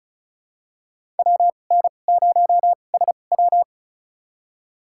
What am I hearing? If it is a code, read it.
Morse code, heard as a keyed tone, about 36 wpm, spaced as for 25 wpm, 700 Hz. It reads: WN0SW
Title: USA Call Signs - 5 Characters - 2x Character Spacing - 35wpm